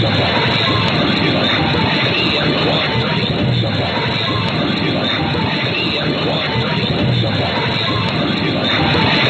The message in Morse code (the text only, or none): none